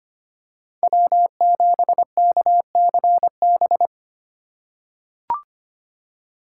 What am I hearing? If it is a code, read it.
Morse code, heard as a keyed tone, about 25 wpm, 700 Hz. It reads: W7KCB
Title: USA Call Signs - 5 Characters - 4x Character Spacing - 25wpm